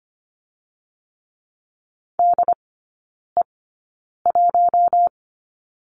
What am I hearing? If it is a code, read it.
Morse code, heard as a keyed tone, about 25 wpm, spaced as for 9 wpm, 700 Hz. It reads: DE1
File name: Lesson 17e - Letter u - Three Characters 25wpm